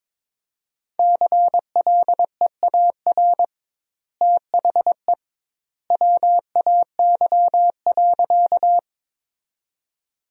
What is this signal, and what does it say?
Morse code, heard as a keyed tone, about 22 wpm, 700 Hz. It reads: CLEAR THE WAY.
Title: 2x Word-Spacing - Sentences from Top 500 Words - 22wpm